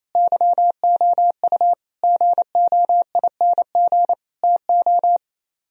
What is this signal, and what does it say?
Morse code, heard as a keyed tone, about 34 wpm, 700 Hz. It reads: YOU GOING TO
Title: Speed-Racing - Sets of 4 Words 28wpm